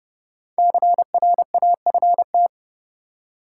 Morse code, heard as a keyed tone, about 29 wpm, 700 Hz. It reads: CRAFT
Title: Five Letter Words - 30wpm